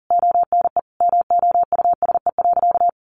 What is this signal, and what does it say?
Morse code, heard as a keyed tone, about 40 wpm, 700 Hz. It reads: ONE MOUSE.
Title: A Sound Of Thunder 40wpm